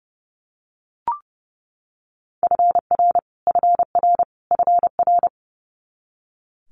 Morse code, heard as a keyed tone, about 30 wpm, 700 Hz. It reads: FR FR FR
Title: Multi-letter Phonemes Repeated 3x 30wpm